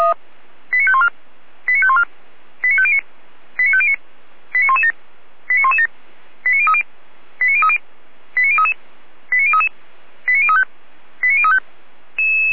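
Five-tone selective call (ZVEI), heard as quick runs of five tones, tones of about 70 ms. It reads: 89514 89514 89509 89509 89117 89117 89030 89030 89033 89033 89035 89035
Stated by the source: ZVEI five-tone call received by SDR.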